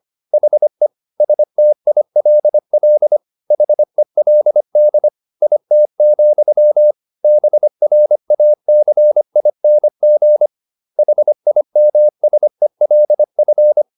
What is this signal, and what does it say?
Morse code, heard as a keyed tone, about 25 wpm, 600 Hz. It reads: HE STILL HELD IT, BRACING HIMSELF
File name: the old man and the sea part 1 25wpm